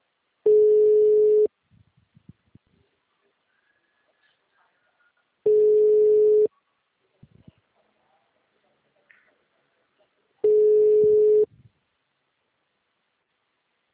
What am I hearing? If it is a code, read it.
Ringback tone (Europe, 425 Hz).